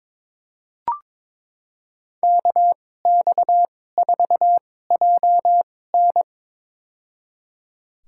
Morse code, heard as a keyed tone, about 22 wpm, spaced as for 16 wpm, 700 Hz. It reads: KX4JN
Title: USA Call Signs - 5 Characters - 2x Character Spacing - 22wpm